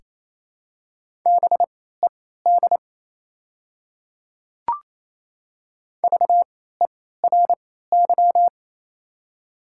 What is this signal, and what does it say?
Morse code, heard as a keyed tone, about 28 wpm, spaced as for 16 wpm, 700 Hz. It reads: BED VERY
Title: 3x Character Spacing - Top 500 Words - 28wpm